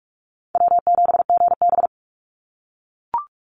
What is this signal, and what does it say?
Morse code, heard as a keyed tone, about 44 wpm, 700 Hz. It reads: W7GB
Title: Easy Call Signs - 45wpm